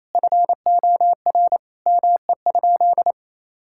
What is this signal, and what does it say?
Morse code, heard as a keyed tone, about 28 wpm, 700 Hz. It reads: FOR ME?
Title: Sentences from Top 1600 Words 28wpm